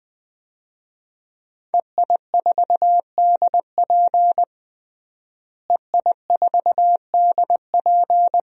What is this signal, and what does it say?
Morse code, heard as a keyed tone, about 21 wpm, 700 Hz. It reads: EI4DP EI4DP
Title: DXCC-100 Call Signs - 5 Characters - Repeated 2x - 20wpm